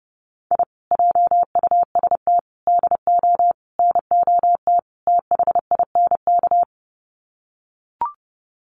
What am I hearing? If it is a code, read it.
Morse code, heard as a keyed tone, about 30 wpm, 700 Hz. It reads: I JUST DO NOT THINK
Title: Sets of 5 Words Encore 30wpm